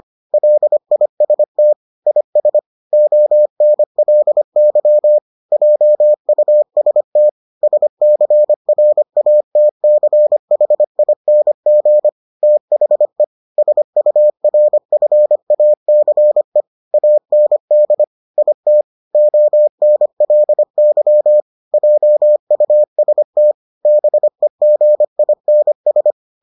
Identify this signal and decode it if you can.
Morse code, heard as a keyed tone, about 25 wpm, 600 Hz. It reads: LIST IS ONLY JUST SCRATCHING THE SURFACE AND IT ONLY JUST BEGINS